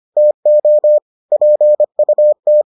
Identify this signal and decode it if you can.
Morse code, heard as a keyed tone, about 25 wpm, 600 Hz. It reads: TO PUT